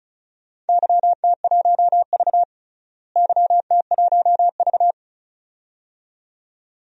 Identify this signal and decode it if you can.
Morse code, heard as a keyed tone, about 36 wpm, 700 Hz. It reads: YT1V YT1V
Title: DXCC-100 Call Signs - 4 Characters - Repeated 2x - 35wpm